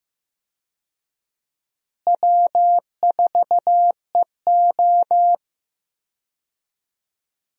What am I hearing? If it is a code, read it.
Morse code, heard as a keyed tone, about 15 wpm, 700 Hz. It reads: W4EO
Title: USA Call Signs - 4 Characters - Repeated 3x - 15wpm